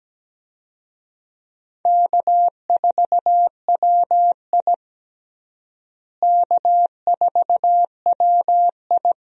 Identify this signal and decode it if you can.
Morse code, heard as a keyed tone, about 17 wpm, 700 Hz. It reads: K4WI K4WI
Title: USA Call Signs - 4 Characters - Repeated 2x - 17wpm